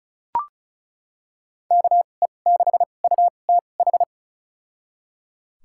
Morse code, heard as a keyed tone, about 34 wpm, spaced as for 25 wpm, 700 Hz. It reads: KE6UTH
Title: USA Call Signs - 6 Characters - 2x Character Spacing - 35wpm